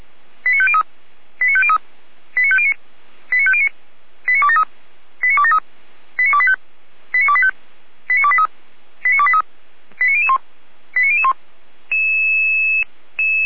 Five-tone selective call (ZVEI), heard as quick runs of five tones, tones of about 70 ms. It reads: 89503 89503 89509 89509 89273 89273 89285 89285 89283 89283 89001 89001